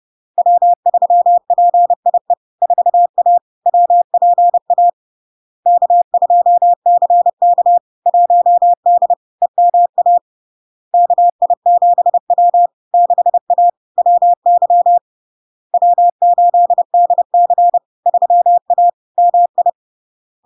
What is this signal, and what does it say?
Morse code, heard as a keyed tone, about 30 wpm, 700 Hz. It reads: W3PIE 4A WPA K2CK 1D EMA KI7W 6A WY W8DC 3A MI
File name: LARK-FD-2022-Log-in-CW-30wpm-700-Hz